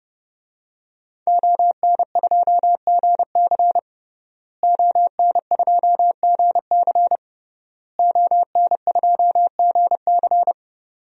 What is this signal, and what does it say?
Morse code, heard as a keyed tone, about 29 wpm, 700 Hz. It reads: ON2GC ON2GC ON2GC
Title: DXCC-100 Call Signs - 5 Characters - Repeated 3x - 30wpm